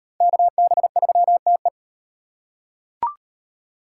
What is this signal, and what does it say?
Morse code, heard as a keyed tone, about 37 wpm, 700 Hz. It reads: KB3TE